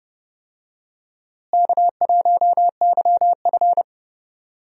Morse code, heard as a keyed tone, about 30 wpm, 700 Hz. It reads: K1YF
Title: USA Call Signs - 4 Characters - Repeated 3x - 30wpm